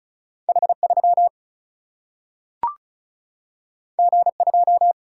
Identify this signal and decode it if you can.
Morse code, heard as a keyed tone, about 36 wpm, 700 Hz. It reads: H3 G2
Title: Single-Letter Single-Number 35wpm